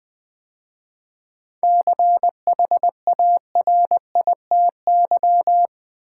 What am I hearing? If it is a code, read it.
Morse code, heard as a keyed tone, about 20 wpm, 700 Hz. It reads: CHARITY
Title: 7 Letter Words - First 3 Letters Given - 20wpm